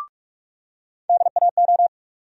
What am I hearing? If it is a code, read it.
Morse code, heard as a keyed tone, about 46 wpm, 700 Hz. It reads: DAO
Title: USA Call Sign Postfixes - 3 Letters 45wpm